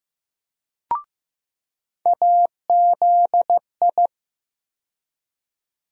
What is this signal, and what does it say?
Morse code, heard as a keyed tone, about 15 wpm, 700 Hz. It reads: AZI
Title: USA Call Sign Postfixes - 3 Letters 15wpm